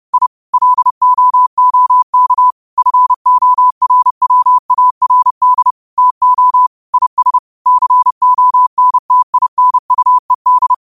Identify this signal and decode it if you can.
Morse code, heard as a keyed tone, about 30 wpm, 1000 Hz. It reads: I LOOK FORWARD TO IS CONTINUED
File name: qotdfiles / 30wpm